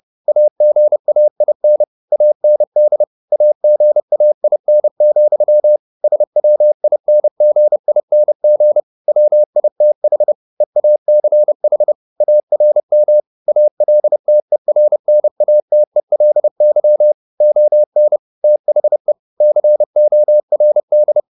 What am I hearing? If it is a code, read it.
Morse code, heard as a keyed tone, about 30 wpm, 600 Hz. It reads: AGAIN AND AGAIN, SWINGING WITH EACH ARM ALTERNATELY ON THE CORD
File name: the old man and the sea / the old man and the sea 1 30wpm